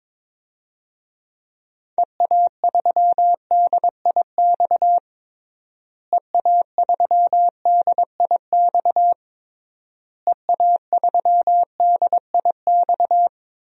Morse code, heard as a keyed tone, about 22 wpm, 700 Hz. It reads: EA3DIX EA3DIX EA3DIX
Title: DXCC-100 Call Signs - 6 Characters - Repeated 3x - 22wpm